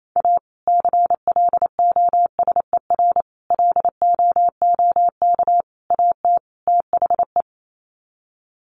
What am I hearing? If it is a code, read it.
Morse code, heard as a keyed tone, about 28 wpm, 700 Hz. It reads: A CLOSER LOOK AT THE
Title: Sets of 5 Words Encore 28wpm